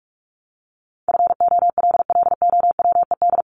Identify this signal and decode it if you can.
Morse code, heard as a keyed tone, about 44 wpm, 700 Hz. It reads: FOLLOWED
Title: ED Suffix - Top 100 Words with Base Words 45wpm